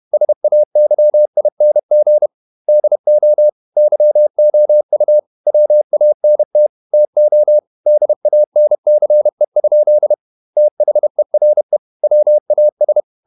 Morse code, heard as a keyed tone, about 31 wpm, 600 Hz. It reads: SAYING DO YOU WANT TO DANCE? THERE WAS
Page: Never 31wpm | CW med Gnister